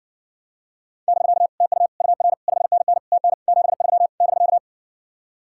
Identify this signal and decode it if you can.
Morse code, heard as a keyed tone, about 67 wpm, 700 Hz. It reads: <BK> TU UR 5NN TN 73 <BK>